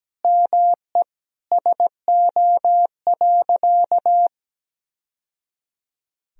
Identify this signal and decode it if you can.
Morse code, heard as a keyed tone, about 17 wpm, 700 Hz. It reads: ME SO.
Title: Sentences from Top 1000 Words - Review Part 3 - 17wpm